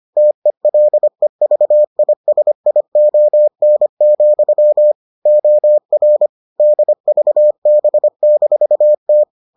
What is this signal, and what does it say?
Morse code, heard as a keyed tone, about 25 wpm, 600 Hz. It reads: TELEVISION, OR DVB-T